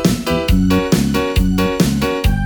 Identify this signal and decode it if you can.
There is no code.